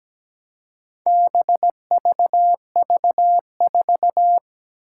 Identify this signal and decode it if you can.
Morse code, heard as a keyed tone, about 17 wpm, 700 Hz. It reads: BVV4
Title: Four Challenging Letter-Number Combinations 17wpm